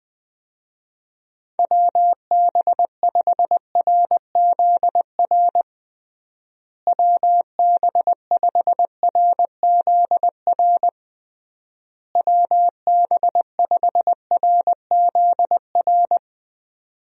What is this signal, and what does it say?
Morse code, heard as a keyed tone, about 20 wpm, 700 Hz. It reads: WB5RZR WB5RZR WB5RZR
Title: USA Call Signs - 6 Characters - Repeated 3x - 20wpm